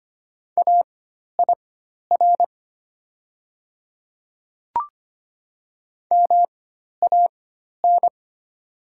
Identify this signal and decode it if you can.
Morse code, heard as a keyed tone, about 25 wpm, spaced as for 12 wpm, 700 Hz. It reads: AIR MAN
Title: 4x Character Spacing - Top 200 Words - 25wpm